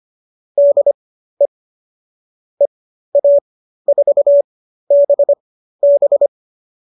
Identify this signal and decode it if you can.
Morse code, heard as a keyed tone, about 25 wpm, spaced as for 13 wpm, 570 Hz. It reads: DE EA4BB